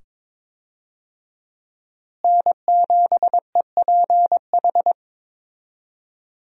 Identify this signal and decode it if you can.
Morse code, heard as a keyed tone, about 22 wpm, 700 Hz. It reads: N7EPH
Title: USA Call Signs - 5 Characters - Repeated 2x - 22wpm